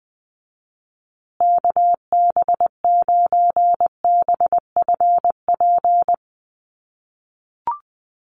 Morse code, heard as a keyed tone, about 20 wpm, 700 Hz. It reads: KB9BFP
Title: USA Call Signs Repeated 3x 20wpm